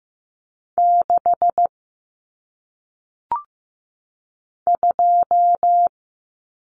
Morse code, heard as a keyed tone, about 15 wpm, 700 Hz. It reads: 6 2
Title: Numbers - 1-Digit 15wpm